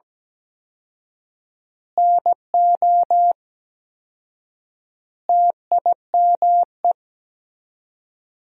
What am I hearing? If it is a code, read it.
Morse code, heard as a keyed tone, about 17 wpm, 700 Hz. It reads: NO TIME
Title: 4x Word-Spacing - Sentences from Top 400 Words - 17wpm